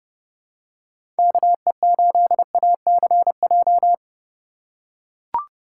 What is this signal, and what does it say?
Morse code, heard as a keyed tone, about 30 wpm, 700 Hz. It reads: KE8ACJ
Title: USA Call Signs - 6 Characters - 2x Character Spacing - 30wpm